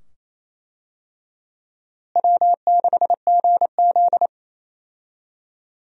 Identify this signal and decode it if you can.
Morse code, heard as a keyed tone, about 28 wpm, 700 Hz. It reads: W6GZ
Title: USA Call Signs - 4 Characters - Repeated 2x - 28wpm